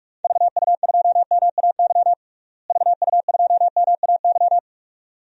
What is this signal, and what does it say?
Morse code, heard as a keyed tone, about 41 wpm, 700 Hz. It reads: VU2MAY VU2MAY
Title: DXCC-100 Call Signs - 6 Characters - Repeated 2x - 45wpm